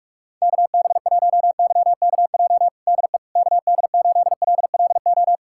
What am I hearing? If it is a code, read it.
Morse code, heard as a keyed tone, about 46 wpm, 700 Hz. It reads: KB1YKJ DE KD8LLO
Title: USA Callsign DE Callsign - 6 Characters - 45wpm